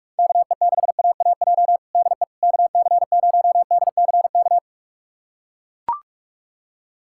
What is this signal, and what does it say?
Morse code, heard as a keyed tone, about 43 wpm, 700 Hz. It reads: KE6AAJ DE KC0DCK
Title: USA Callsign DE Callsign - 6 Characters - 45wpm